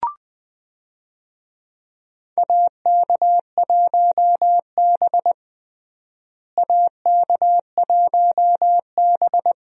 Morse code, heard as a keyed tone, about 21 wpm, 700 Hz. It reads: AK1B AK1B
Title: USA Call Signs - 4 Characters - Repeated 2x - 20wpm